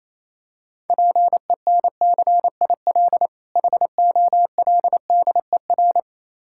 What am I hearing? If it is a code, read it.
Morse code, heard as a keyed tone, about 27 wpm, 700 Hz. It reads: PENCIL HOLDER
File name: Easy Words - Speed-Racing with Hints - 22wpm